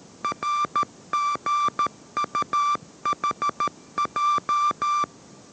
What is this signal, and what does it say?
Morse code, heard as a keyed tone, about 13 wpm, 1180 Hz. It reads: RGUHJ
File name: 14wpm